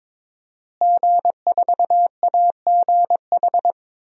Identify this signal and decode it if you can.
Morse code, heard as a keyed tone, about 21 wpm, 700 Hz. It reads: G4AGH
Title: DXCC-100 Call Signs - 5 Characters - Repeated 3x - 22wpm